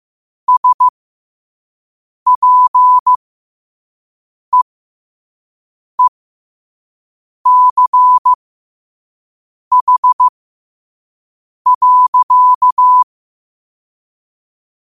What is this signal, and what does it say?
Morse code, heard as a keyed tone, about 15 wpm, spaced as for 5 wpm, 1000 Hz. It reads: SPEECH.
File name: qotdfiles / 5wpm